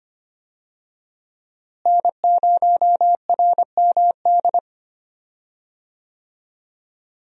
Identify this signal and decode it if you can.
Morse code, heard as a keyed tone, about 25 wpm, 700 Hz. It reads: N0RMD
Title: USA Call Signs - 5 Characters - Repeated 3x - 25wpm